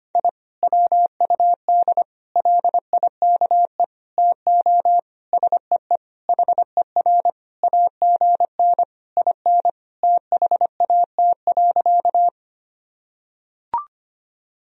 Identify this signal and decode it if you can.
Morse code, heard as a keyed tone, about 25 wpm, 700 Hz. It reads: I WUD LIKE TO SEE HER AGN IN THAT.
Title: Sentences from Top 200 Words and Abbreviations 25wpm